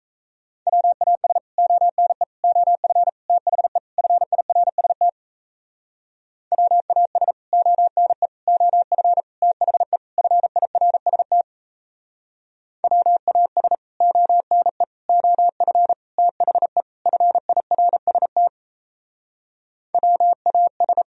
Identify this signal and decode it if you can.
Morse code, heard as a keyed tone, about 36 wpm, 700 Hz. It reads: WAS ONE OF THE FIRST WAS ONE OF THE FIRST WAS ONE OF THE FIRST WAS